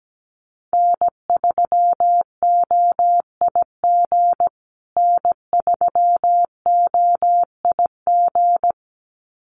Morse code, heard as a keyed tone, about 17 wpm, 700 Hz. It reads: N3OIG N3OIG
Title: USA Call Signs Repeated 2x 17wpm